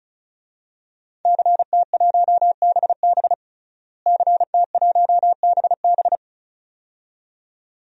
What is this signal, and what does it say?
Morse code, heard as a keyed tone, about 36 wpm, 700 Hz. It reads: CT1BB CT1BB
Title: DXCC-100 Call Signs - 5 Characters - Repeated 2x - 35wpm